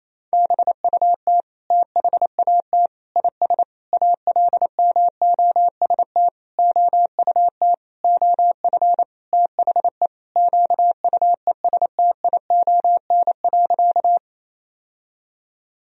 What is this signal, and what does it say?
Morse code, heard as a keyed tone, about 27 wpm, 700 Hz. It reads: BUT THAT IS ALMOST OUT OF THE QUESTION.
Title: Sentences from Top 1000 Words - Review Part 4 - 28wpm